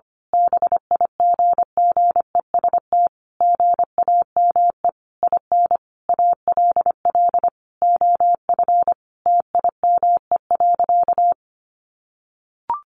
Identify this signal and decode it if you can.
Morse code, heard as a keyed tone, about 25 wpm, 700 Hz. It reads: BIGGEST GAME IN ALL OF TIME.
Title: A Sound Of Thunder 25wpm